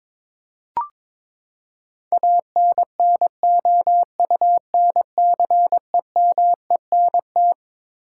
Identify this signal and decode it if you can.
Morse code, heard as a keyed tone, about 21 wpm, 700 Hz. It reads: ANNOUNCEMENT
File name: Twelve Letter Words - 22wpm